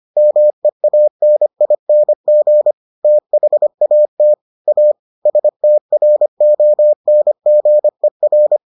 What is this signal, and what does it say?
Morse code, heard as a keyed tone, about 25 wpm, 600 Hz. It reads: MEANING THAT A STRONGER